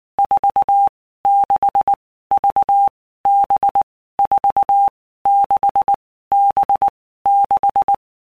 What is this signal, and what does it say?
Morse code, heard as a keyed tone, about 19 wpm, 800 Hz. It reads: 4 6 V B 4 6 B 6
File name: Basic Letter Groups By Blocks / Groups@18 / VB46@18